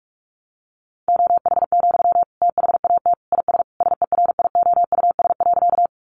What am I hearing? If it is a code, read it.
Morse code, heard as a keyed tone, about 44 wpm, 700 Hz. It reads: OH, THAT IS SERIOUS.
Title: Sentences from Top 1100 Words 45wpm